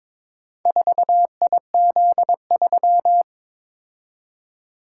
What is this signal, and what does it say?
Morse code, heard as a keyed tone, about 22 wpm, 700 Hz. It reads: 4IZ3
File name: practice / 22wpm / Four Letter-Number Combinations - 22wpm